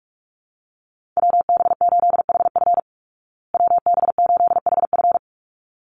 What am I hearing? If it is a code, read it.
Morse code, heard as a keyed tone, about 44 wpm, 700 Hz. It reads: WB8HF WB8HF
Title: Call Signs Repeated 3x - 45wpm